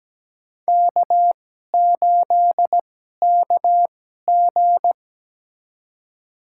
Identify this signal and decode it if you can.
Morse code, heard as a keyed tone, about 17 wpm, spaced as for 12 wpm, 700 Hz. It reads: K8KG
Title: USA Call Signs - 4 Characters - 2x Character Spacing - 17wpm